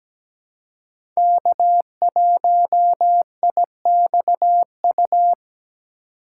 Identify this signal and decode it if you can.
Morse code, heard as a keyed tone, about 17 wpm, 700 Hz. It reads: K1IXU
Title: USA Call Signs - 5 Characters - 2x Character Spacing - 17wpm